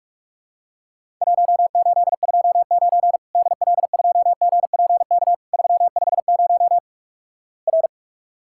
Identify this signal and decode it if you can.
Morse code, heard as a keyed tone, about 46 wpm, 700 Hz. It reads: 1829 DL2GPK 350 R
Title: Worked All Europe DX Contest - QTC Exchange - 45wpm